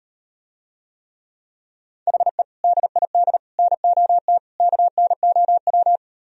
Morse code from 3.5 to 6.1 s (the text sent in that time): NOT KNOW